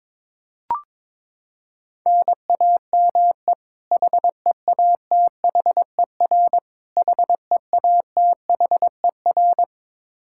Morse code, heard as a keyed tone, about 22 wpm, 700 Hz. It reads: NAME HEATHER HEATHER
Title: K1USN SST QSO Practice 22wpm